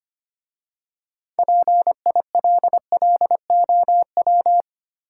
Morse code, heard as a keyed tone, about 25 wpm, 700 Hz. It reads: PILLOW
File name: practice / 25wpm / Easy Words with Hints - 25wpm